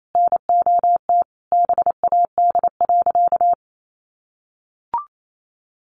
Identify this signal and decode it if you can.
Morse code, heard as a keyed tone, about 28 wpm, 700 Hz. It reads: NOT BAD.
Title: Sentences from Top 700 Words 28wpm